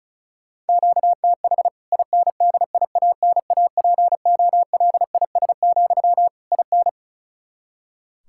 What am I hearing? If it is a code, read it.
Morse code, heard as a keyed tone, about 34 wpm, 700 Hz. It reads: QTH INDIANAPOLIS, IN
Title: K1USN SST QSO Practice 35wpm